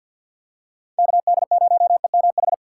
Morse code, heard as a keyed tone, about 52 wpm, 700 Hz. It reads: KD0EMH
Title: USA Call Signs - 6 Characters - 4x Character Spacing - 50wpm